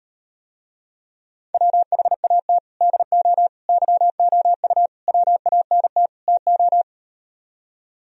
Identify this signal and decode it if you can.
Morse code, heard as a keyed tone, about 39 wpm, 700 Hz. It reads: WHAT DO YOU WANT TO